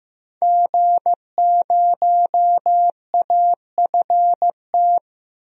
Morse code, heard as a keyed tone, about 15 wpm, 700 Hz. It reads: G0AFT
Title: DXCC-100 Call Signs - 5 Characters - 15wpm